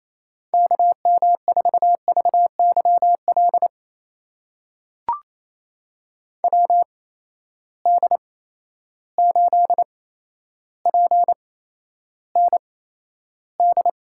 Morse code, heard as a keyed tone, about 28 wpm, spaced as for 12 wpm, 700 Hz. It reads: KM4VYL WD8PND
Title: USA Call Signs - 6 Characters - 8x Character Spacing - 28wpm